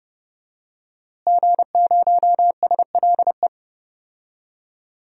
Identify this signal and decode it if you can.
Morse code, heard as a keyed tone, about 30 wpm, 700 Hz. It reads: G0SLE
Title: DXCC-100 Call Signs - 5 Characters - Repeated 2x - 30wpm